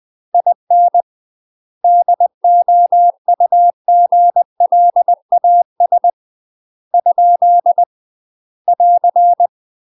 Morse code, heard as a keyed tone, about 20 wpm, 700 Hz. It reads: IN DOUGLAS ? <AR>
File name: UK MASTER QSOs TRAINING / 20WPM / UK MASTER QSOs 0066